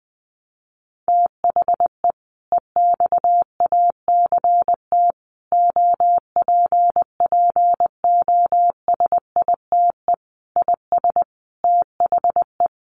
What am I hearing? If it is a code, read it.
Morse code, heard as a keyed tone, about 20 wpm, 700 Hz. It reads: THE EXACT OPPOSITE IS THE
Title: Sentences from Top 1300 Words 20wpm